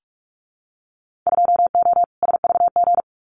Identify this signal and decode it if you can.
Morse code, heard as a keyed tone, about 46 wpm, 700 Hz. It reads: 2O SVG
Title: Winter Field Day Exchanges 45wpm